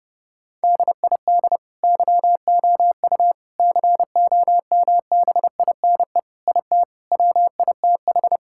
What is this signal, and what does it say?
Morse code, heard as a keyed tone, about 30 wpm, 700 Hz. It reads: DID YOU COMBINE IT WITH
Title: Sentences from Top 1600 Words 30wpm